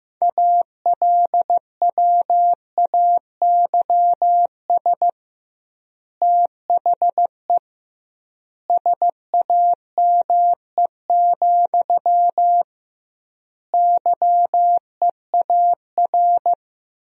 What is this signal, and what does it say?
Morse code, heard as a keyed tone, about 15 wpm, 700 Hz. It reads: ALWAYS THE SAME, YEAR
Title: 2x Word-Spacing - Sentences from Top 300 Words - 15wpm